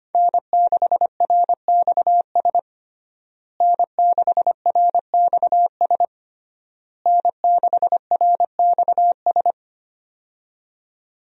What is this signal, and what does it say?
Morse code, heard as a keyed tone, about 25 wpm, 700 Hz. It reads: N6RXS N6RXS N6RXS
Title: USA Call Signs - 5 Characters - Repeated 3x - 25wpm